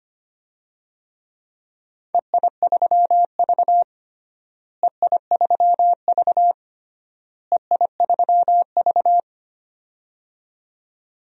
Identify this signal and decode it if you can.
Morse code, heard as a keyed tone, about 25 wpm, 700 Hz. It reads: EI3V EI3V EI3V
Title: DXCC-100 Call Signs - 4 Characters - Repeated 3x - 25wpm